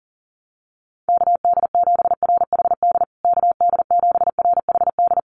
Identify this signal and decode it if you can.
Morse code, heard as a keyed tone, about 40 wpm, 700 Hz. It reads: KD7RHD KD7RHD
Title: USA Call Signs Repeated 3x 40wpm